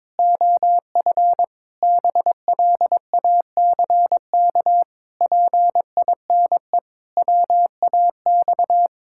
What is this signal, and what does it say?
Morse code, heard as a keyed tone, about 22 wpm, 700 Hz. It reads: OF BLACK PINE WAX
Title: Pangram Challenge v2 - 22wpm